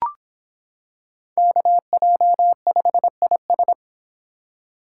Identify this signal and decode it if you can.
Morse code, heard as a keyed tone, about 25 wpm, 700 Hz. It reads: KJ5IS